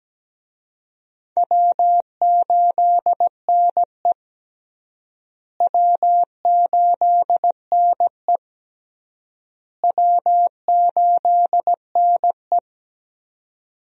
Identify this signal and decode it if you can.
Morse code, heard as a keyed tone, about 17 wpm, 700 Hz. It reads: W8NE W8NE W8NE
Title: USA Call Signs - 4 Characters - Repeated 3x - 17wpm